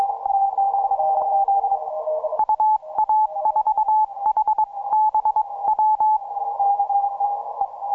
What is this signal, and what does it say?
Morse code, heard as a keyed tone, about 22 wpm, spaced as for 19 wpm, 860 Hz. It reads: UA4HBW
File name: simple-file-list / FW2CW 17MCW